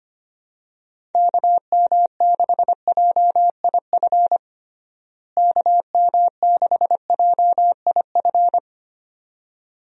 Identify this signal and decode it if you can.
Morse code, heard as a keyed tone, about 25 wpm, 700 Hz. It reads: KM6JIF KM6JIF
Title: USA Call Signs - 6 Characters - Repeated 2x - 25wpm